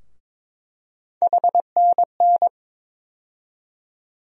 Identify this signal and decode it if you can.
Morse code, heard as a keyed tone, about 21 wpm, 700 Hz. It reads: HNN
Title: USA Call Sign Postfixes - 3 Letters 22wpm